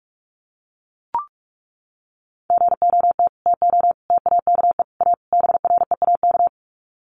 Morse code, heard as a keyed tone, about 44 wpm, 700 Hz. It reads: GOT TO TAKE A BREAK
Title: Sets of 5 Words Encore 45wpm